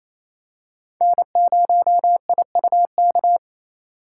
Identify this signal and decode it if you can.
Morse code, heard as a keyed tone, about 28 wpm, 700 Hz. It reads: N0IUK
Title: USA Call Signs - 4x Character Spacing - 28wpm